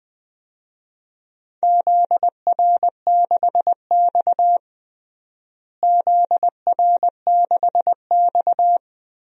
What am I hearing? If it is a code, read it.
Morse code, heard as a keyed tone, about 21 wpm, 700 Hz. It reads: ZR6X ZR6X
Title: DXCC-100 Call Signs - 4 Characters - Repeated 3x - 20wpm